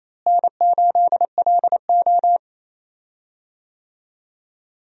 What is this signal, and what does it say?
Morse code, heard as a keyed tone, about 27 wpm, 700 Hz. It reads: N8LO
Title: USA Call Signs - 4 Characters - Repeated 2x - 28wpm